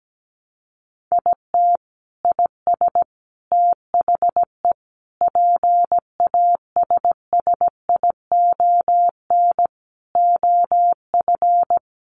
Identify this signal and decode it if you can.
Morse code, heard as a keyed tone, about 17 wpm, 700 Hz. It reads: IT IS THE PASSION OF
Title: Sentences from Top 1300 Words 17wpm